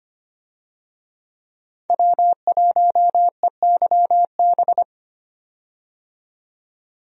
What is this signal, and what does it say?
Morse code, heard as a keyed tone, about 25 wpm, 700 Hz. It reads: W1EYB
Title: USA Call Signs - 5 Characters - 25wpm